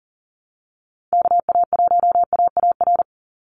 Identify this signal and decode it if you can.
Morse code, heard as a keyed tone, about 40 wpm, 700 Hz. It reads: KA1AAR
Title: USA Call Signs 40wpm